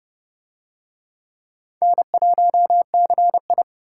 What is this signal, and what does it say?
Morse code, heard as a keyed tone, about 29 wpm, 700 Hz. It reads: N1CI